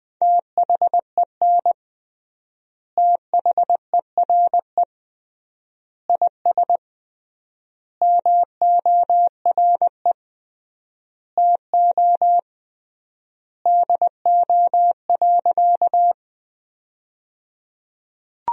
Morse code, heard as a keyed tone, about 20 wpm, 700 Hz. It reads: THEN THERE IS MORE TO DO.
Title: 3x Word-Spacing - Sentences from Top 100 Words - 20wpm